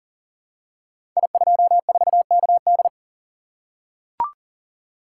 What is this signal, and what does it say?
Morse code, heard as a keyed tone, about 41 wpm, 700 Hz. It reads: I24KD
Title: Code Groups - 5 Letter:Numbers - 40wpm